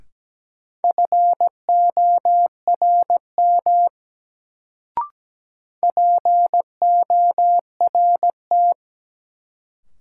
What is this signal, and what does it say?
Morse code, heard as a keyed tone, about 17 wpm, 700 Hz. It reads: FORM PORT
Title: Sentences from Top 1000 Words - Review Part 1 - New Word Drill - 17wpm